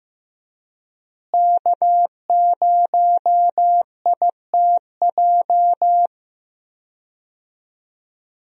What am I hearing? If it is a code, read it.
Morse code, heard as a keyed tone, about 16 wpm, 700 Hz. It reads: K0ITJ
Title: USA Call Signs - 5 Characters - Repeated 3x - 15wpm